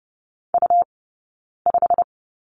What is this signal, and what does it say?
Morse code, heard as a keyed tone, about 30 wpm, 700 Hz. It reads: U 5
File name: Lesson 18d - Letter m - Two Characters 30wpm